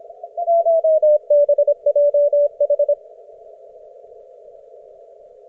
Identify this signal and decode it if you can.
Morse code, heard as a keyed tone, about 25 wpm, 560 Hz. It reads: TBJH